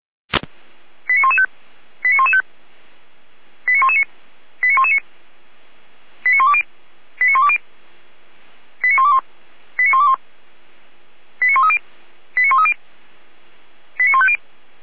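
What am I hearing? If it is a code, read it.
Five-tone selective call (ZVEI), heard as quick runs of five tones, tones of about 70 ms. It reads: 89115 89115 89119 89119 89120 89120 89121 89121 89130 89130 89150
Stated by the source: ZVEI five-tone selective call received on SDR, Feuerwehr Orlamünde audio archive.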